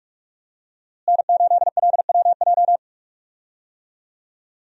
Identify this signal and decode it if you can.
Morse code, heard as a keyed tone, about 42 wpm, 700 Hz. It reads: N8LWJ
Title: USA Call Signs - 5 Characters - Repeated 3x - 45wpm